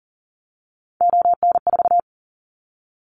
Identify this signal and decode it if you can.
Morse code, heard as a keyed tone, about 40 wpm, 700 Hz. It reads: ON4
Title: Callsign Prefixes for Top 100 DXCC - 40wpm